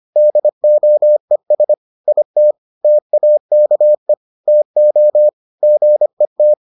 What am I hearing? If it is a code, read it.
Morse code, heard as a keyed tone, about 25 wpm, 600 Hz. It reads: DOES IT TAKE TO GET